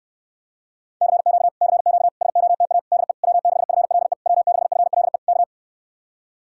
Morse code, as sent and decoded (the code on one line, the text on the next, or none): -.-. --.- / -.-. --.- / ... --- - .- / -.. / .-- -.... ..-. .-.. / .-- -.... ..-. .-.. / -.-
CQ CQ SOTA DE W6FLE W6FLE K